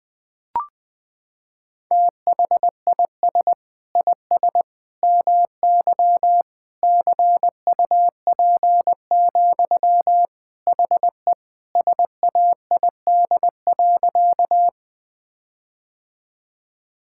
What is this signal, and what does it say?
Morse code, heard as a keyed tone, about 20 wpm, 700 Hz. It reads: THIS IS MY CUP, HE SAID.
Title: Sentences from Top 1600 Words 20wpm